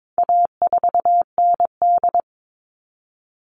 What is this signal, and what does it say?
Morse code, heard as a keyed tone, about 22 wpm, 700 Hz. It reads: A4ND
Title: All Cut Numbers - 4-Digit 22wpm